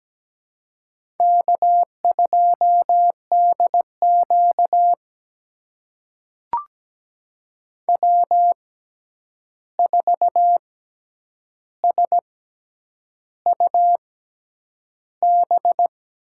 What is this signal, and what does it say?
Morse code, heard as a keyed tone, about 17 wpm, spaced as for 8 wpm, 700 Hz. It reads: K2DQ W4SUB
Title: USA Call Signs - 6x Character Spacing - 17wpm